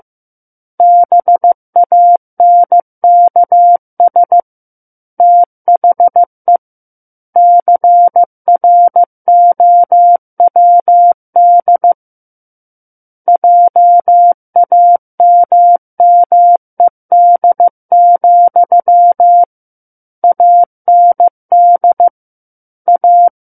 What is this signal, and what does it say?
Morse code, heard as a keyed tone, about 15 wpm, 700 Hz. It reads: BANKS THE CROWD JAMMED, AND A
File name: Book One / 06-Chapter 6 - 15 WPM